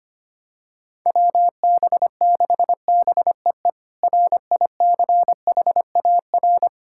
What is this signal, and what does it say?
Morse code, heard as a keyed tone, about 25 wpm, 700 Hz. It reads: WB6BEE RICHAR
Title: K1USN SST QSO Practice - 25wpm